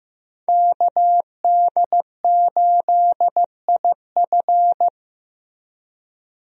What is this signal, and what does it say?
Morse code, heard as a keyed tone, about 15 wpm, 700 Hz. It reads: KD8IF
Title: USA Call Signs - 5 Characters - Repeated 2x - 15wpm